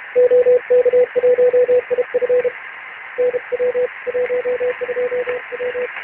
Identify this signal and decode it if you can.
Morse code, heard as a keyed tone, about 33 wpm, 500 Hz. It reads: OK1IF NW 12W
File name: ARCHIV / BPSK CW / OK1IF-OK1ANG- -1230CET-3766.5-3767